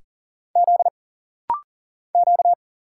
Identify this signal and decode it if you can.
Morse code, heard as a keyed tone, about 40 wpm, 700 Hz. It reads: Z Q
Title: Single Letter-Number Alt - Rapid-Fire 40wpm